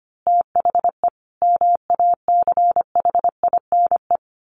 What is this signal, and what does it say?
Morse code, heard as a keyed tone, about 25 wpm, 700 Hz. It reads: THE MACHINE
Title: A Sound Of Thunder 25wpm